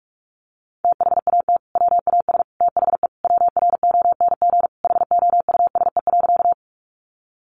Morse code, heard as a keyed tone, about 44 wpm, 700 Hz. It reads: THAT WAS THE WRONG HOUSE.
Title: Sentences from Top 900 Words 45wpm